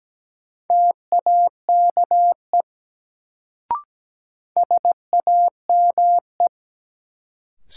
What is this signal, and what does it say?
Morse code, heard as a keyed tone, about 17 wpm, 700 Hz. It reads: TAKE SAME
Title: Sets of 2 Words - New Word Drill - 17wpm